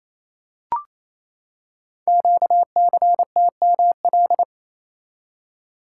Morse code, heard as a keyed tone, about 29 wpm, 690 Hz. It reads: QCTML
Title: Code Groups - 5 Letters - 28wpm